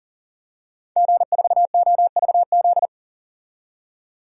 Morse code, heard as a keyed tone, about 40 wpm, 700 Hz. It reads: G4OVZ
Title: DXCC-100 Call Signs - 5 Characters - 40wpm